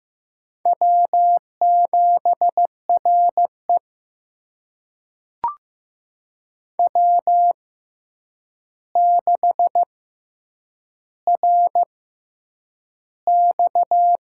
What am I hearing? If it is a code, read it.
Morse code, heard as a keyed tone, about 15 wpm, spaced as for 8 wpm, 700 Hz. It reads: W7RE W6RX
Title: USA Call Signs - 4 Characters - 6x Character Spacing - 15wpm